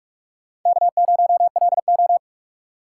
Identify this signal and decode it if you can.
Morse code, heard as a keyed tone, about 45 wpm, 700 Hz. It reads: K0LO
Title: USA Call Signs - 2x Character Spacing - 45wpm